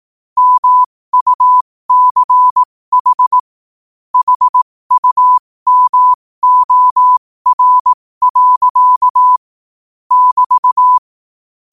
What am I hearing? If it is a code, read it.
Morse code, heard as a keyed tone, about 18 wpm, spaced as for 15 wpm, 1000 Hz. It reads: MUCH HUMOR. <BT>
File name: qotdfiles / 15wpm